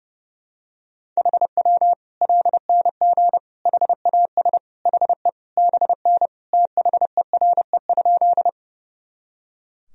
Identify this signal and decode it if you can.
Morse code, heard as a keyed tone, about 30 wpm, 700 Hz. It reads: HW LNG HAS HE BN THERE?
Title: Sentences from Top 100 Words and Abbreviations 30wpm